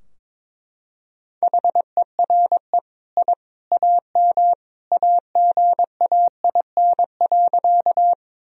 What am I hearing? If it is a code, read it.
Morse code, heard as a keyed tone, about 21 wpm, 700 Hz. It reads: HERE I AM AGAIN.
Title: Sentences from Top 1000 Words - Review Part 2 - 22wpm